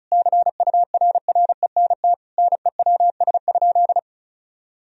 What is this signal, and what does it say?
Morse code, heard as a keyed tone, about 34 wpm, 700 Hz. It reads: CURRENT NEWS?
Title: Sentences from Top 1000 Words - Review Part 4 - 35wpm